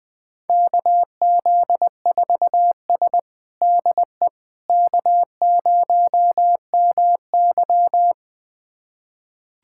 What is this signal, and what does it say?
Morse code, heard as a keyed tone, about 21 wpm, 700 Hz. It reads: KZ4S DE K0MY
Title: USA Callsign DE Callsign - 4 Characters - 20wpm